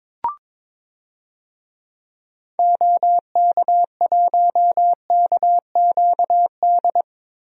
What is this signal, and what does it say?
Morse code, heard as a keyed tone, about 22 wpm, 700 Hz. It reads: OK1KQD
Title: DXCC-100 Call Signs - 6 Characters - 22wpm